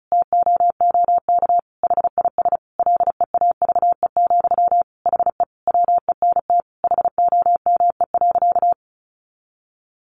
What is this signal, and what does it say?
Morse code, heard as a keyed tone, about 34 wpm, 700 Hz. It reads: TOOK HIS LEAVE, HE WENT HOME.
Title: Sentences from Top 1200 Words 35wpm